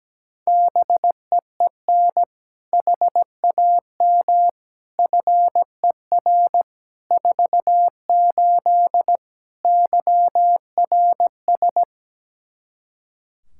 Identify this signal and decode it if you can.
Morse code, heard as a keyed tone, about 17 wpm, 700 Hz. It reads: BEEN HAM FER 48 YRS
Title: Practice QSO - Exchange 3 - More Info - 17wpm